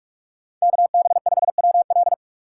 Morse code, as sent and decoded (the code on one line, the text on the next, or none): -.- -... ..... .-- .-..
KB5WL